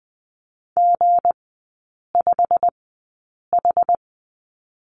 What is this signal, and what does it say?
Morse code, heard as a keyed tone, about 20 wpm, spaced as for 8 wpm, 700 Hz. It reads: G5H
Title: 159 Lesson 26e - Letter g - Three Characters 20wpm